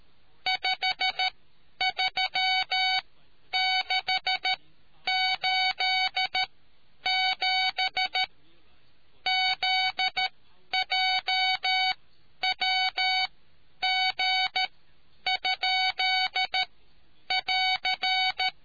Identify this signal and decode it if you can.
Morse code, heard as a keyed tone, about 13 wpm, spaced as for 9 wpm, 750 Hz. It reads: 53687 ZJWG?<AR>